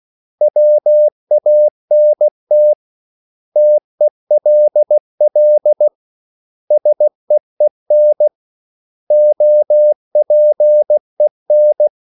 Morse code, heard as a keyed tone, about 16 wpm, 600 Hz. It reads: WANT TELL SEEN OPEN
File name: CW / Practice / 500MostCommonEnglishWords16